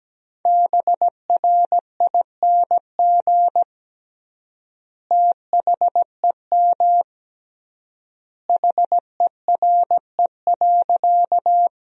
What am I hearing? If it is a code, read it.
Morse code, heard as a keyed tone, about 17 wpm, 700 Hz. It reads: BRING THEM HERE.
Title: 3x Word-Spacing - Sentences from Top 500 Words - 17wpm